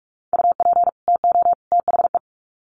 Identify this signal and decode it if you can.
Morse code, heard as a keyed tone, about 44 wpm, 700 Hz. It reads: UP TO THE
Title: Sets of 3 Words 45wpm